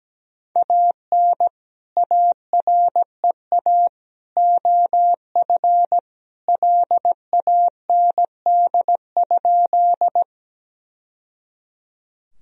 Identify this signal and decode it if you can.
Morse code, heard as a keyed tone, about 17 wpm, 700 Hz. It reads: AN AREA OF LAND?
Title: Sentences from Top 1000 Words - Review Part 2 - 17wpm